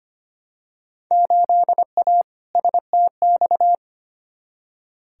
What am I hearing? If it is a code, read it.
Morse code, heard as a keyed tone, about 25 wpm, 700 Hz. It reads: 8A STX
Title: Field Day Exchanges Encore - 25wpm